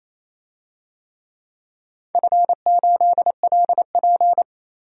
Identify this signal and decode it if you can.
Morse code, heard as a keyed tone, about 27 wpm, 700 Hz. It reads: F8LP
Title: DXCC-100 Call Signs - 4 Characters - 28wpm